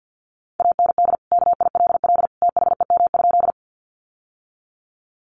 Morse code, heard as a keyed tone, about 50 wpm, 700 Hz. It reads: AND KILL THEM?
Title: Sentences from Top 1400 Words 50wpm